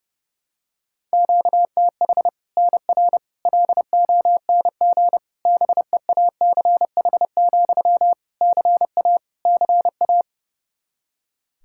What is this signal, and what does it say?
Morse code, heard as a keyed tone, about 32 wpm, 700 Hz. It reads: QTH NR LONG BEACH, CA CA
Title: K1USN SST QSO Practice 30wpm